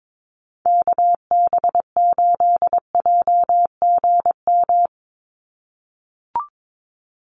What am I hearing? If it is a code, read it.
Morse code, heard as a keyed tone, about 22 wpm, 700 Hz. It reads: KB8JGM
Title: USA Call Signs 22wpm